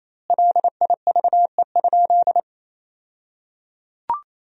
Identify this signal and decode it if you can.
Morse code, heard as a keyed tone, about 27 wpm, 700 Hz. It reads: LIVE?
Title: 2x Word-Spacing - Sentences from Top 200 Words - 28wpm